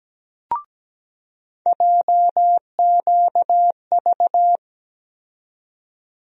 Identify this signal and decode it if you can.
Morse code, heard as a keyed tone, about 17 wpm, 700 Hz. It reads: JQV
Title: USA Call Sign Postfixes - 3 Letters 17wpm